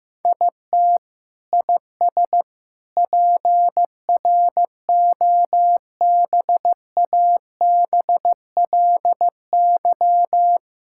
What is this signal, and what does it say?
Morse code, heard as a keyed tone, about 15 wpm, 700 Hz. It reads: IT IS PROBABLY
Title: Sentences from Top 1000 Words - Review Part 6 - 15wpm